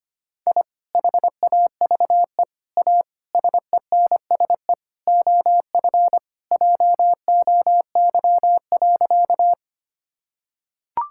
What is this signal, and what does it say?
Morse code, heard as a keyed tone, about 25 wpm, 700 Hz. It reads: I HAVE A SENSE OF JOY.
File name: practice / 25wpm / Sentences with 3-Word Phrases - 25wpm